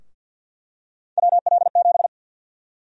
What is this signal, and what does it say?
Morse code, heard as a keyed tone, about 49 wpm, 700 Hz. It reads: WL7
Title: USA Call Sign Prefixes - Repeated 2x - 50wpm